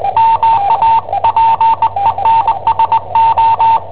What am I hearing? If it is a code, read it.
Morse code, heard as a keyed tone, about 20 wpm, 900 Hz, with radio noise. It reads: MA PENSO
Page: Cassetta esercizi preparata da I7OHP per IW5ELP